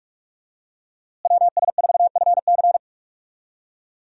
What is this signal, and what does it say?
Morse code, heard as a keyed tone, about 41 wpm, 700 Hz. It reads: WS4FC
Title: USA Call Signs - 5 Characters - Repeated 3x - 45wpm